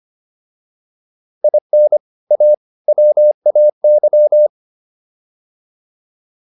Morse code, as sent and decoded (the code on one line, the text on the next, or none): .. -. / .- / .-- .- -.--
IN A WAY